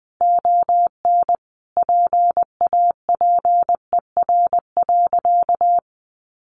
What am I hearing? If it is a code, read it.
Morse code, heard as a keyed tone, about 20 wpm, 700 Hz. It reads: ON PAPER.
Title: Sentences from Top 1200 Words 20wpm